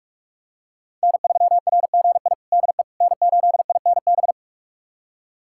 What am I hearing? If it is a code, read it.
Morse code, heard as a keyed tone, about 42 wpm, 700 Hz. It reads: N3RGI DE N8INB
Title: USA Callsign DE Callsign - 5 Characters - 45wpm